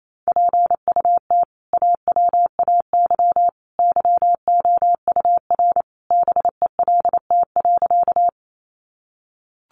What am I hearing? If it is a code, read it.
Morse code, heard as a keyed tone, about 28 wpm, 700 Hz. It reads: PUT AWAY YOUR BELT.
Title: Sentences from Top 1400 Words 28wpm